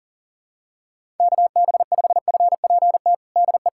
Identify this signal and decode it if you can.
Morse code, heard as a keyed tone, about 40 wpm, 700 Hz. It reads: KB5FPT DE
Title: USA Callsign DE Callsign - 6 Characters - 40wpm